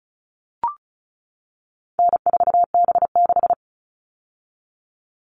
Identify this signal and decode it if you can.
Morse code, heard as a keyed tone, about 34 wpm, 700 Hz. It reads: N4B6
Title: All Cut Numbers - 4-Digit 35wpm